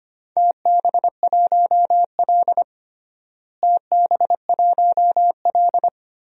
Morse code, heard as a keyed tone, about 25 wpm, 700 Hz. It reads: TB1L TB1L
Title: DXCC-100 Call Signs - 4 Characters - Repeated 2x - 25wpm